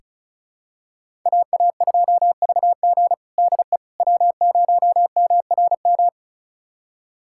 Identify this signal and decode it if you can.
Morse code, heard as a keyed tone, about 36 wpm, 700 Hz. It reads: AA2VG DE W0MRM
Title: USA Callsign DE Callsign - 5 Characters - 35wpm